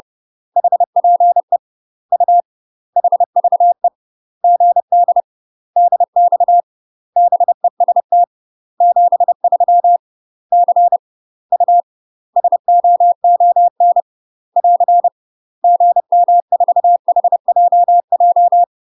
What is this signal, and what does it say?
Morse code, heard as a keyed tone, about 30 wpm, 700 Hz. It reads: HPE U HVE GD DX BEST 73 C U SOON <AR> GM4HJJ
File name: UK MASTER QSOs TRAINING / 30WPM / UK MASTER QSOs 0050